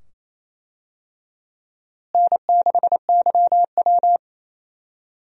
Morse code, heard as a keyed tone, about 28 wpm, 700 Hz. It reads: N6YW
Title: USA Call Signs - 4 Characters - Repeated 2x - 28wpm